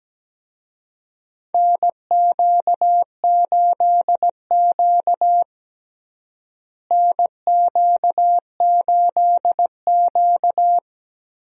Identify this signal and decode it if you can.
Morse code, heard as a keyed tone, about 17 wpm, 700 Hz. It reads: NQ8Q NQ8Q
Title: USA Call Signs - 4 Characters - Repeated 2x - 17wpm